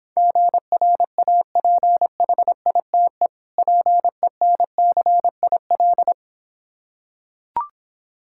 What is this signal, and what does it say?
Morse code, heard as a keyed tone, about 26 wpm, 700 Hz. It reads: GRAPHITE PENCIL